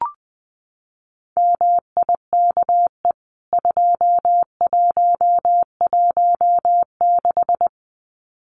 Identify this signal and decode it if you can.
Morse code, heard as a keyed tone, about 20 wpm, 700 Hz. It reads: MIKE 2116
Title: CWT Exchange 20wpm